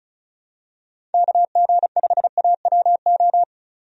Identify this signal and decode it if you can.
Morse code, heard as a keyed tone, about 34 wpm, 700 Hz. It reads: KG5AWO
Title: USA Call Signs - 6 Characters - 35wpm